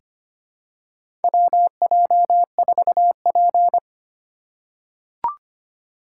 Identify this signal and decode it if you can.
Morse code, heard as a keyed tone, about 25 wpm, 700 Hz. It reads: WJ4P
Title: USA Call Signs - 4 Characters - 2x Character Spacing - 25wpm